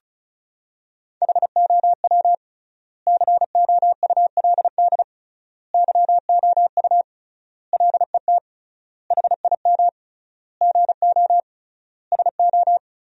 Morse code, heard as a keyed tone, about 36 wpm, 700 Hz. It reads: HOW COULD YOU LET HIM GO SO
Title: 3x Word-Spacing - Sentences from Top 300 Words - 35wpm